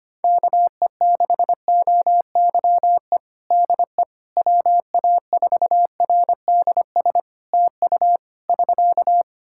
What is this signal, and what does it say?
Morse code, heard as a keyed tone, about 25 wpm, 700 Hz. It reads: KE6OYE DE WA4RDS TU <SK>
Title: Practice QSO - Exchange 4 - Ending - 25wpm